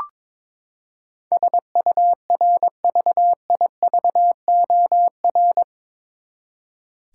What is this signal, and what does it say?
Morse code, heard as a keyed tone, about 21 wpm, 700 Hz. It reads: SURVIVOR
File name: practice / 22wpm / Eight Letter Words - 22wpm